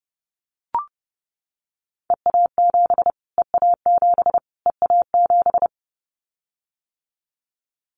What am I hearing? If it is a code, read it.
Morse code, heard as a keyed tone, about 30 wpm, 690 Hz. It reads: EA7 EA7 EA7
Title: Callsign Prefixes for Top 25 DXCC - Repeated 3x - 30wpm